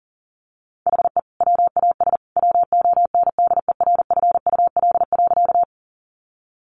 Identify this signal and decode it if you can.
Morse code, heard as a keyed tone, about 40 wpm, 700 Hz. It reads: HE WAS WONDERFUL.
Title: Sentences from Top 1100 Words 40wpm